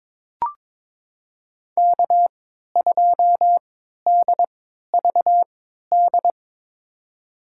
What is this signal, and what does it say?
Morse code, heard as a keyed tone, about 22 wpm, spaced as for 13 wpm, 700 Hz. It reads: K2DVD
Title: USA Call Signs - 5 Characters - 3x Character Spacing - 22wpm